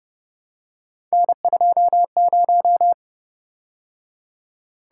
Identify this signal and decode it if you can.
Morse code, heard as a keyed tone, about 30 wpm, 700 Hz. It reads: N20